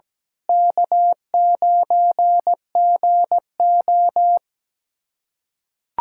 Morse code, heard as a keyed tone, about 17 wpm, 700 Hz. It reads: K9GO
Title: USA Callsign DE Callsign - 4 Characters - 17wpm